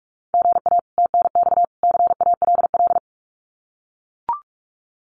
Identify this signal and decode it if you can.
Morse code, heard as a keyed tone, about 44 wpm, 700 Hz. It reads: GA TNX CALL
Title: Common QSO Phrases 45wpm